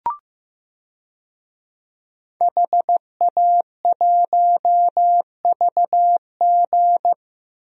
Morse code, heard as a keyed tone, about 15 wpm, 700 Hz. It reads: HA1VG
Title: DXCC-100 Call Signs - 5 Characters - 15wpm